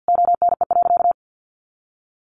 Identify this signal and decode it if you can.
Morse code, heard as a keyed tone, about 48 wpm, 700 Hz. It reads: ONE.
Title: Sentences from Top 1400 Words 50wpm